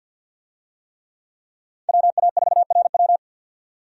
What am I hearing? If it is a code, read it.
Morse code, heard as a keyed tone, about 52 wpm, 700 Hz. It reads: WA4RW
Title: USA Call Signs - 5 Characters - 50wpm